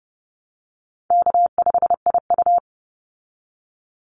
Morse code, heard as a keyed tone, about 30 wpm, 700 Hz. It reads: K5IU
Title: Call Signs Repeated 2x - 30wpm